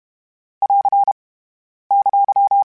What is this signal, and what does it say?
Morse code, heard as a keyed tone, about 32 wpm, 800 Hz. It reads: <AR> !
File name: (spaced) / FromSMs spaced mp3s / Antennas sp@30